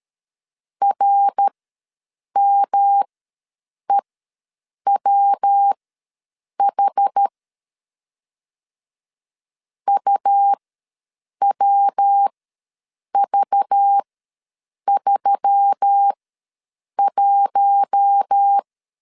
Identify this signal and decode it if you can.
Morse code, heard as a keyed tone, about 13 wpm, spaced as for 7 wpm, 800 Hz. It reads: RMEWH UWV31
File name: LEZIONI / 6 PKX0 / 6 PK ws 12wpm 3116